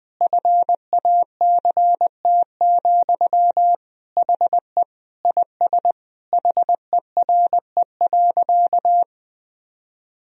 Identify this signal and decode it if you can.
Morse code, heard as a keyed tone, about 21 wpm, 700 Hz. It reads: FACT, HE IS HERE.
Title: Sentences from Top 1000 Words - Review Part 3 - 20wpm